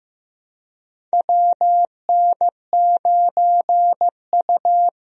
Morse code, heard as a keyed tone, about 16 wpm, 700 Hz. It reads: WN9U
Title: USA Call Signs - 4 Characters - Repeated 2x - 15wpm